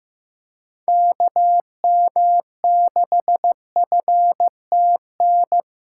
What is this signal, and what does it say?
Morse code, heard as a keyed tone, about 15 wpm, 700 Hz. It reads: KM6FTN
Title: USA Callsign DE Callsign 15wpm